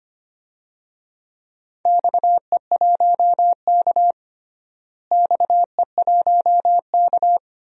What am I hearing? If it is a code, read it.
Morse code, heard as a keyed tone, about 25 wpm, 700 Hz. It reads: XE1K XE1K
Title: DXCC-100 Call Signs - 4 Characters - Repeated 2x - 25wpm